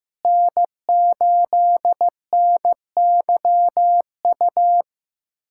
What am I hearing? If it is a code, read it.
Morse code, heard as a keyed tone, about 15 wpm, 700 Hz. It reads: N8NYU
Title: USA Callsign DE Callsign 15wpm